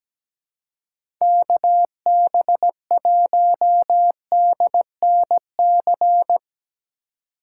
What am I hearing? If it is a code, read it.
Morse code, heard as a keyed tone, about 17 wpm, 700 Hz. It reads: KB1DNC
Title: USA Call Signs - 6 Characters - 4x Character Spacing - 17wpm